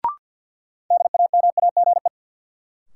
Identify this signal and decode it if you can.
Morse code, heard as a keyed tone, about 52 wpm, 700 Hz. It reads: DAMAGE
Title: Sentences from Top 1100 Words - New Word Drill - 50wpm